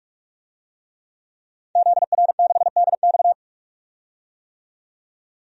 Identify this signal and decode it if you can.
Morse code, heard as a keyed tone, about 46 wpm, 700 Hz. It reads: ZR6DX
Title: DXCC-100 Call Signs - 5 Characters - 45wpm